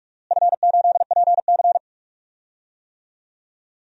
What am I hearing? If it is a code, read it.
Morse code, heard as a keyed tone, about 48 wpm, 700 Hz. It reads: F8PC
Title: DXCC-100 Call Signs - 4 Characters - Repeated 3x - 45wpm